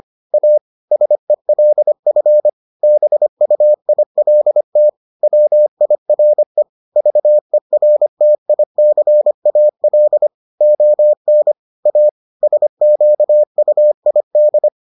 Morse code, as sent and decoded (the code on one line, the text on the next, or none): .- / ... .-.. ..-. / -... ..- .. .-.. - / .-- .. .-. / ...- . .-. - .. -.-. .- .-.. / --- -. / .- / ... --.- ..- .. -..
A SELF BUILT WIRE VERTICAL ON A SQUID